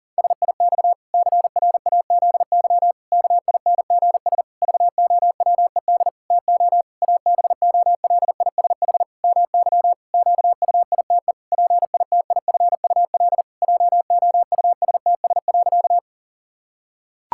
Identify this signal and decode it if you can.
Morse code, heard as a keyed tone, about 40 wpm, 700 Hz. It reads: SIX CRAZY KINGS VOWED TO ABOLISH MY QUITE PITIFUL JOUSTS.
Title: Pangram Challenge v2 - 40wpm